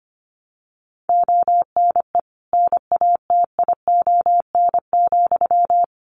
Morse code, heard as a keyed tone, about 25 wpm, 700 Hz. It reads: ONE NATION,
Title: Sentences from Top 900 Words 25wpm